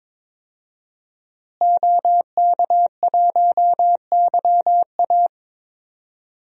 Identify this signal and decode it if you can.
Morse code, heard as a keyed tone, about 22 wpm, 700 Hz. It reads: OK1YA
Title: DXCC-100 Call Signs - 5 Characters - Repeated 2x - 22wpm